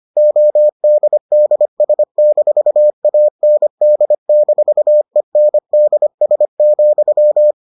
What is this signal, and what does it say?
Morse code, heard as a keyed tone, about 25 wpm, 600 Hz. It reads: ODDS-AND-ENDS,